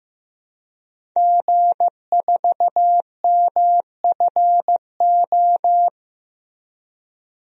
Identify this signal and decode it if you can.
Morse code, heard as a keyed tone, about 16 wpm, 700 Hz. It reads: G4MFO
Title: DXCC-100 Call Signs - 5 Characters - Repeated 2x - 15wpm